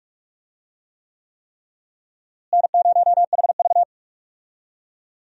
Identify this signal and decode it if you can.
Morse code, heard as a keyed tone, about 41 wpm, 700 Hz. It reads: N0HV
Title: USA Call Signs - 4 Characters - Repeated 3x - 45wpm